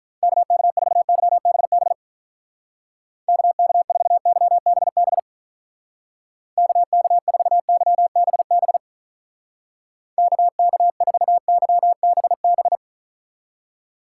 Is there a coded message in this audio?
Yes: Morse code from 0.2 to 12.8 s, heard as a keyed tone, about 45 wpm, 700 Hz. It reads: KK4YBB KK4YBB KK4YBB KK4YBB